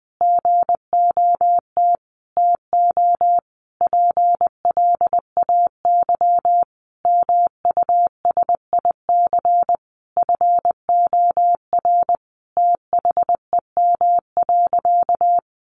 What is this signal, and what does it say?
Morse code, heard as a keyed tone, about 20 wpm, 700 Hz. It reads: GOT TO PLAY MUSIC FOR THEM.
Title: Sentences from Top 300 Words 20wpm